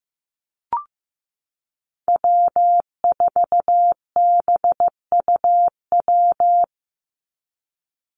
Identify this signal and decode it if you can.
Morse code, heard as a keyed tone, about 15 wpm, 700 Hz. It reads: W4BUW
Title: Call Signs Repeated 2x - 15wpm